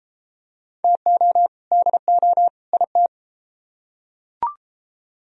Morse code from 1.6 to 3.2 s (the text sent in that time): DO IT